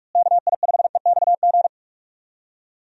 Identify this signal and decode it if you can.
Morse code, heard as a keyed tone, about 46 wpm, 700 Hz. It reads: KI5EXG
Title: USA Call Signs - 6 Characters - Repeated 2x - 45wpm